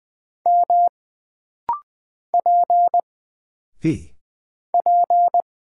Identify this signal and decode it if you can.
Morse code, heard as a keyed tone, about 20 wpm, 700 Hz. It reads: M P P
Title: Single Character (Letter:Number:Punctuation:Prosign) - Rapid-Fire - 20wpm